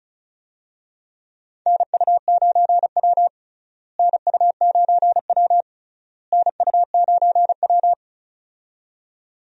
Morse code, heard as a keyed tone, about 36 wpm, 700 Hz. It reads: NU9W NU9W NU9W
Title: USA Call Signs - 4 Characters - Repeated 3x - 35wpm